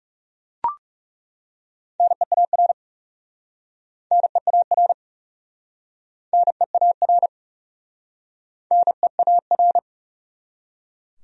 Morse code, heard as a keyed tone, about 40 wpm, 700 Hz. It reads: NEAR NEAR NEAR NEAR